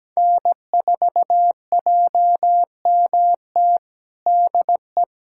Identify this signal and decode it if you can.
Morse code, heard as a keyed tone, about 17 wpm, 700 Hz. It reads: N4JMT DE
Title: USA Callsign DE Callsign - 5 Characters - 17wpm